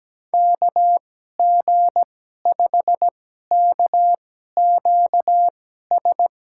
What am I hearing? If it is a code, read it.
Morse code, heard as a keyed tone, about 17 wpm, spaced as for 12 wpm, 700 Hz. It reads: KG5KQS
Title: USA Call Signs - 2x Character Spacing - 17wpm